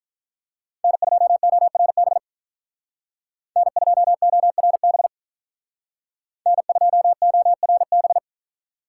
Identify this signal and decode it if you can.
Morse code, heard as a keyed tone, about 44 wpm, 700 Hz. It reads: N2ORB N2ORB N2ORB